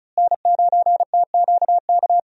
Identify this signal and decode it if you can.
Morse code, heard as a keyed tone, about 34 wpm, 700 Hz. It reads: N9TQK
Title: USA Call Signs - 5 Characters - Repeated 3x - 35wpm